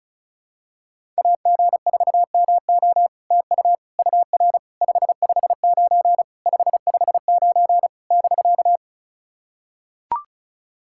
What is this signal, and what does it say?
Morse code, heard as a keyed tone, about 36 wpm, 700 Hz. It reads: AG4MO TU UR 559 559 <BK>
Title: SOTA QSO Practice 35wpm